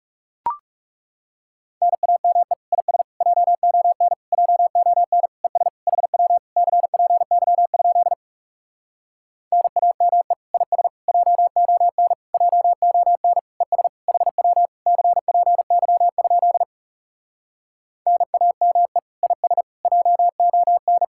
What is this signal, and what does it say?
Morse code, heard as a keyed tone, about 40 wpm, 700 Hz. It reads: NAME IS JON JON ES HW CPY? NAME IS JON JON ES HW CPY? NAME IS JON